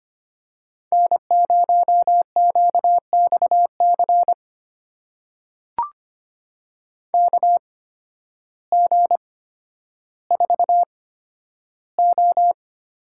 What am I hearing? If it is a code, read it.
Morse code, heard as a keyed tone, about 25 wpm, spaced as for 12 wpm, 700 Hz. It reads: N0QXC KG4O
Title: USA Call Signs - 8x Character Spacing - 25wpm